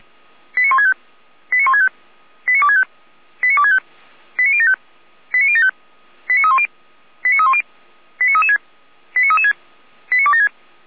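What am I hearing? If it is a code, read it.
Five-tone selective call (ZVEI), heard as quick runs of five tones, tones of about 70 ms. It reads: 89165 89165 89265 89265 89075 89075 89310 89310 89336 89336 89276
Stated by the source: ZVEI five-tone call received by SDR, from the feuerwehr-orlamuende.org fire brigade audio archive.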